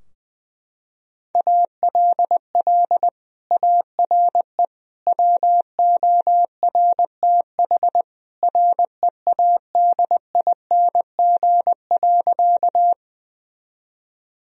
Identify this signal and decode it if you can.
Morse code, heard as a keyed tone, about 19 wpm, 700 Hz. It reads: ALL ARE WORTH READING.
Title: Sentences from Top 1000 Words - Review Part 3 - 20wpm